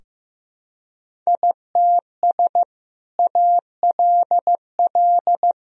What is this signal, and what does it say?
Morse code, heard as a keyed tone, about 15 wpm, 700 Hz. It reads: ITS ALL
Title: Sentences from Top 1000 Words - Review Part 2 - 15wpm